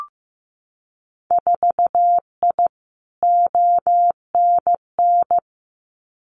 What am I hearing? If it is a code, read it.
Morse code, heard as a keyed tone, about 15 wpm, 700 Hz. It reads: 4I ONN
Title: Winter Field Day Exchanges Encore 15wpm